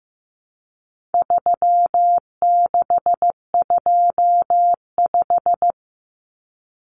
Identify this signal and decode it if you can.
Morse code, heard as a keyed tone, about 15 wpm, 700 Hz. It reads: 3625
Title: Common Cut Numbers - 4-Digit 15wpm